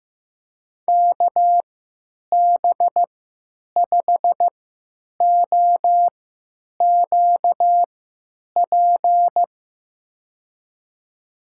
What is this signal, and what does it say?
Morse code, heard as a keyed tone, about 15 wpm, spaced as for 8 wpm, 700 Hz. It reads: KB5OQP
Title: USA Call Signs - 3x Character Spacing - 15wpm